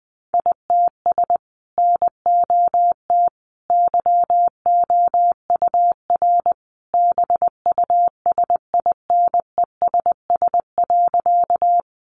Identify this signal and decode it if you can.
Morse code, heard as a keyed tone, about 20 wpm, 700 Hz. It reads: ITS NOT YOUR BUSINESS.
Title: Sentences from Top 600 Words 20wpm